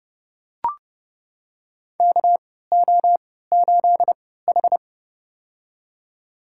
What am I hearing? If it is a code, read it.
Morse code, heard as a keyed tone, about 30 wpm, spaced as for 17 wpm, 700 Hz. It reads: KO8H
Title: USA Call Signs - 4 Characters - 3x Character Spacing - 30wpm